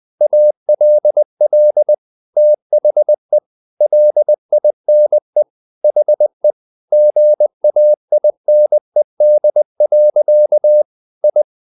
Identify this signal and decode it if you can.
Morse code, heard as a keyed tone, about 20 wpm, 600 Hz. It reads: ALL THE LINE HE GAINED. I
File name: the old man and the sea part 2 20wpm